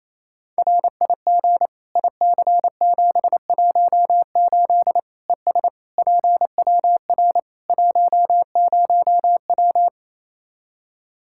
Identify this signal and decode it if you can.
Morse code, heard as a keyed tone, about 28 wpm, 700 Hz. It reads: RIG IC718 ES PWR 10W
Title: Practice QSO - Exchange 2 - Info - 28wpm